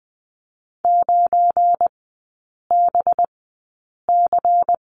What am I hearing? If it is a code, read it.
Morse code, heard as a keyed tone, about 20 wpm, spaced as for 8 wpm, 700 Hz. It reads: 9BC
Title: 184 Lesson 30e - Letter b - Three Characters 20wpm